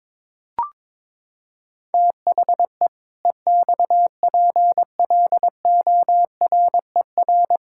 Morse code, heard as a keyed tone, about 22 wpm, 700 Hz. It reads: THE EXPLORER
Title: Pangram Challenge v2 - 22wpm